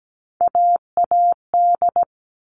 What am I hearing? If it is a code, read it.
Morse code, heard as a keyed tone, about 17 wpm, 700 Hz. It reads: AAD
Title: All Cut Numbers - 3-Digit 17wpm